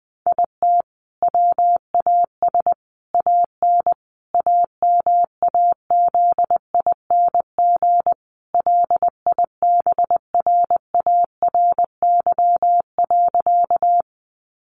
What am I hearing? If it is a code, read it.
Morse code, heard as a keyed tone, about 20 wpm, 700 Hz. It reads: IT WAS AN AMAZING LIBRARY.
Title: Sentences from Top 1500 Words 20wpm